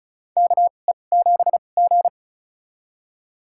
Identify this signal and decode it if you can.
Morse code, heard as a keyed tone, about 36 wpm, spaced as for 25 wpm, 700 Hz. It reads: KE7G
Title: USA Call Signs - 4 Characters - 2x Character Spacing - 35wpm